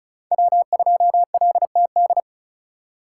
Morse code, heard as a keyed tone, about 36 wpm, 700 Hz. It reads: W2LTD
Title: USA Call Signs - 5 Characters - 4x Character Spacing - 35wpm